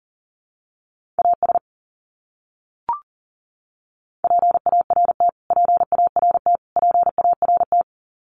Morse code, heard as a keyed tone, about 40 wpm, 700 Hz. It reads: AS PART PART PART
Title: Top 200 Words Repeated 40wpm